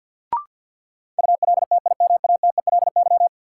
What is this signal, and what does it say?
Morse code, heard as a keyed tone, about 53 wpm, 700 Hz. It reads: ULTIMATELY
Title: Sentences from Top 1200 Words - New Word Drill - 50wpm